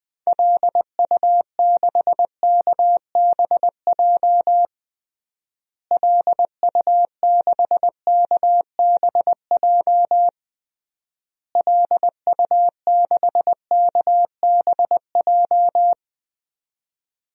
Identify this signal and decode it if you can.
Morse code, heard as a keyed tone, about 20 wpm, 700 Hz. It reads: LU6KBJ LU6KBJ LU6KBJ
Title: DXCC-100 Call Signs - 6 Characters - Repeated 3x - 20wpm